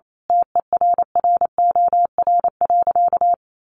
Morse code, heard as a keyed tone, about 28 wpm, 700 Hz. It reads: TERROR.
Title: A Sound Of Thunder 28wpm